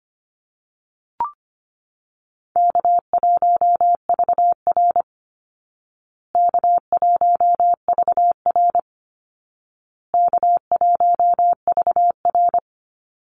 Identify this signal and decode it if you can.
Morse code, heard as a keyed tone, about 25 wpm, 700 Hz. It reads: K1VR K1VR K1VR
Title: Easy Call Signs Repeated 3x - 25wpm